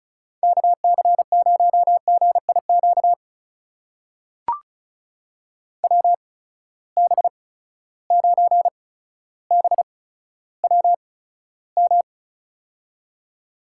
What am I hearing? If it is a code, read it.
Morse code, heard as a keyed tone, about 36 wpm, spaced as for 15 wpm, 700 Hz. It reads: KC0GIQ WB9BWM
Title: USA Call Signs - 6 Characters - 8x Character Spacing - 35wpm